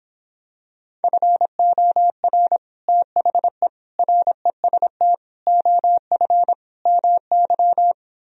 Morse code, heard as a keyed tone, about 26 wpm, 700 Hz. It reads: FOR THE REST OF MY